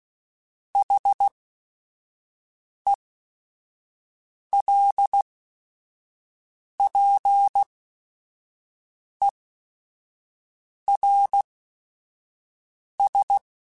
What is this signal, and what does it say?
Morse code, heard as a keyed tone, about 16 wpm, spaced as for 5 wpm, 800 Hz. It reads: HELPERS